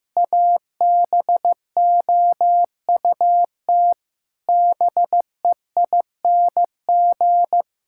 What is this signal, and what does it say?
Morse code, heard as a keyed tone, about 15 wpm, 700 Hz. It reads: ABOUT BEING
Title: Sentences from Top 1000 Words - Review Part 6 - 15wpm